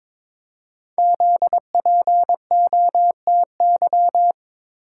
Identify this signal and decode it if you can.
Morse code, heard as a keyed tone, about 22 wpm, 700 Hz. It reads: ZPOTY
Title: Code Groups - 5 Letters - 22wpm